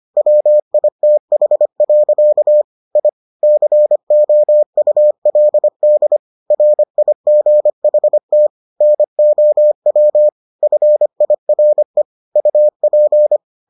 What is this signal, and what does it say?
Morse code, heard as a keyed tone, about 25 wpm, 600 Hz. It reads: WITH. I COULD RIGHT NOW FIRE UP